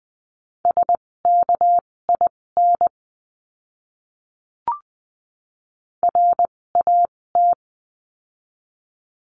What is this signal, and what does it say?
Morse code, heard as a keyed tone, about 20 wpm, spaced as for 16 wpm, 700 Hz. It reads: SKIN RAT
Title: 256 Lesson 42a - Words 12-20wpm